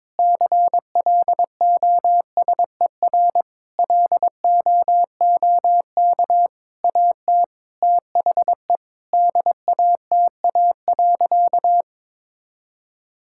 Morse code, heard as a keyed tone, about 22 wpm, 700 Hz. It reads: CLOSER LOOK AT THE DATA.
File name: Sentences with 7-Word Phrases - 22wpm